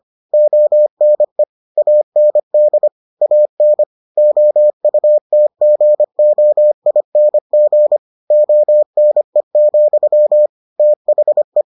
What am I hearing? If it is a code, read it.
Morse code, heard as a keyed tone, about 25 wpm, 600 Hz. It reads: ONE AND AN OUTGOING ONE, THE